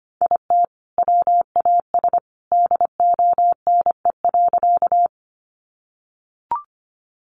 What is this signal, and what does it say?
Morse code, heard as a keyed tone, about 25 wpm, 700 Hz. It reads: IT WAS DONE.
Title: Sentences from Top 500 Words 25wpm